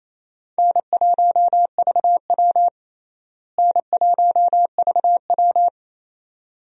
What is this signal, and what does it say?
Morse code, heard as a keyed tone, about 28 wpm, 700 Hz. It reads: N1VW N1VW
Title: USA Call Signs - 4 Characters - Repeated 2x - 28wpm